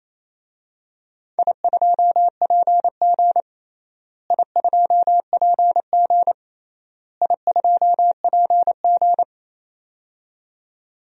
Morse code, heard as a keyed tone, about 28 wpm, 700 Hz. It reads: I2PG I2PG I2PG
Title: DXCC-100 Call Signs - 4 Characters - Repeated 3x - 28wpm